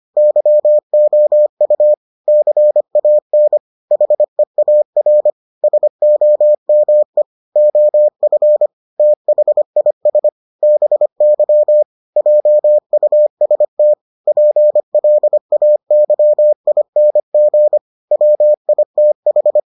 Morse code, heard as a keyed tone, about 25 wpm, 600 Hz. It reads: YOU CAN HEAR SOME OF THIS BY JUST PLAYING WITH